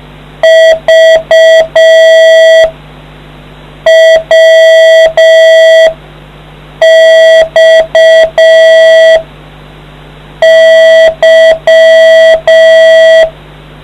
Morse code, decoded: V W X Y

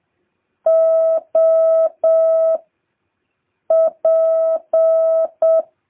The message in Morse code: O P

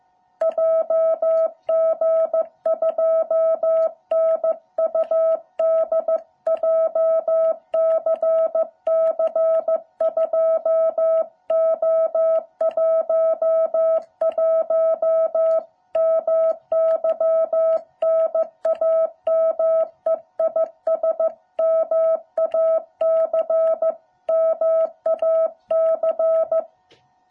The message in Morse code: JG2NUD JCC 2O11 MY NAME IS MAC MAC

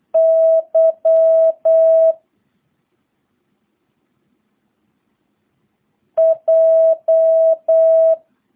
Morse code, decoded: Y J

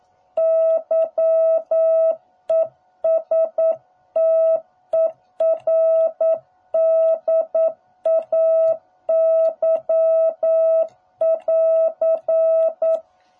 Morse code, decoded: YESTERDAY <AR>

